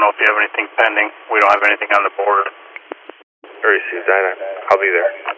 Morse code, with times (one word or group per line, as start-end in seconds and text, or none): none